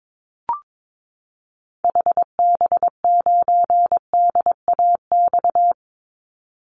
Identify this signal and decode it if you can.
Morse code, heard as a keyed tone, about 22 wpm, 700 Hz. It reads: HB9DAX
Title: Call Signs Repeated 2x - 22wpm